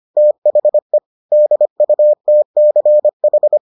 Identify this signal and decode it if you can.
Morse code, heard as a keyed tone, about 25 wpm, 600 Hz. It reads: THE DUTCH